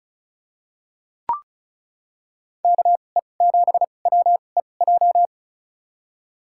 Morse code, heard as a keyed tone, about 36 wpm, spaced as for 25 wpm, 700 Hz. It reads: KE7WEJ
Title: USA Call Signs - 6 Characters - 2x Character Spacing - 35wpm